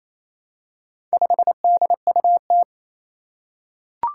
Morse code, heard as a keyed tone, about 27 wpm, 700 Hz. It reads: 5DUT
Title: Four Challenging Letter-Number Combinations 28wpm